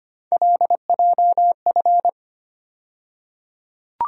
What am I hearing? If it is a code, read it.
Morse code, heard as a keyed tone, about 25 wpm, 700 Hz. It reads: LJF
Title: USA Call Sign Postfixes - 3 Letters 25wpm